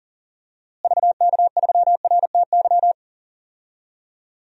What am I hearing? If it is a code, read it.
Morse code, heard as a keyed tone, about 40 wpm, 700 Hz. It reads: VK3RTY
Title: DXCC-100 Call Signs - 6 Characters - Repeated 3x - 40wpm